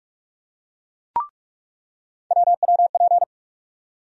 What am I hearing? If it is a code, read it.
Morse code, heard as a keyed tone, about 46 wpm, 700 Hz. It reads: WWP
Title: USA Call Sign Postfixes - 3 Letters 45wpm